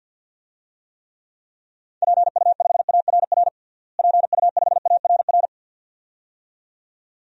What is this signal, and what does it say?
Morse code, heard as a keyed tone, about 51 wpm, 700 Hz. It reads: PU5ARR PU5ARR
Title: DXCC-100 Call Signs - 6 Characters - Repeated 2x - 50wpm